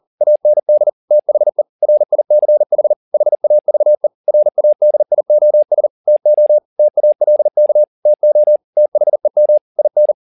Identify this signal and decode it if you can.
Morse code, heard as a keyed tone, about 40 wpm, 600 Hz. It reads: AND THE RICH HAVE RADIOS TO TALK TO THEM IN